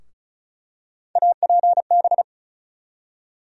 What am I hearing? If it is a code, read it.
Morse code, heard as a keyed tone, about 36 wpm, 700 Hz. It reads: APB
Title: USA Call Sign Postfixes - 3 Letters 35wpm